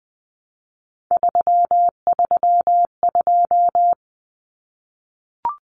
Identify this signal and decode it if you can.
Morse code, heard as a keyed tone, about 20 wpm, 700 Hz. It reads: 332
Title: Common Cut Numbers - 3-Digit 20wpm